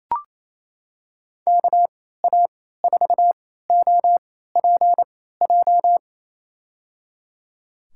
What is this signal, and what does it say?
Morse code, heard as a keyed tone, about 27 wpm, spaced as for 16 wpm, 700 Hz. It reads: KA4OPJ
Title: USA Call Signs - 3x Character Spacing - 28wpm